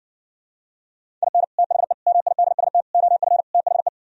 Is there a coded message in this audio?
Yes: Morse code from 1.2 to 3.9 s, heard as a keyed tone, about 68 wpm, 700 Hz. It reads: IN THE MIDST OF THE